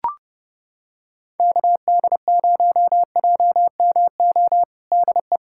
Morse code, heard as a keyed tone, about 30 wpm, 700 Hz. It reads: KD0JMO DE
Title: USA Callsign DE Callsign - 6 Characters - 30wpm